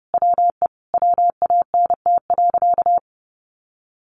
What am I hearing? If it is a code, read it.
Morse code, heard as a keyed tone, about 30 wpm, 700 Hz. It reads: WE WANT.
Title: Sentences from Top 200 Words 30wpm